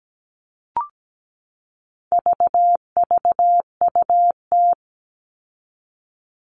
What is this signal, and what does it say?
Morse code, heard as a keyed tone, about 17 wpm, 700 Hz. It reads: VVUT
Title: All Cut Numbers - 4-Digit 17wpm